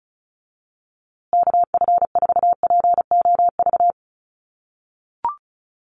Morse code, heard as a keyed tone, about 34 wpm, 700 Hz. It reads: KF4POV
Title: USA Call Signs Repeated 3x 35wpm